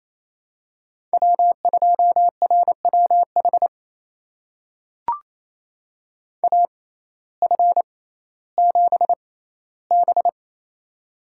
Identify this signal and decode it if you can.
Morse code, heard as a keyed tone, about 28 wpm, spaced as for 15 wpm, 700 Hz. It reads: W2RWH AF7B
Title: USA Call Signs - 6x Character Spacing - 28wpm